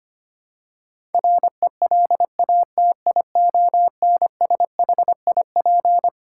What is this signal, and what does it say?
Morse code, heard as a keyed tone, about 25 wpm, 700 Hz. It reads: RELATIONSHIP
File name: Twelve Letter Words - 25wpm